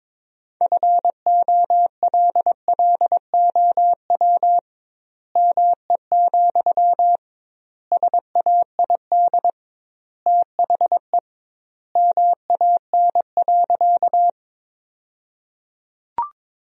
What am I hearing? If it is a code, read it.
Morse code, heard as a keyed tone, about 22 wpm, 700 Hz. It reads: FOLLOW ME, SAID THE MAN.
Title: 2x Word-Spacing - Sentences from Top 200 Words - 22wpm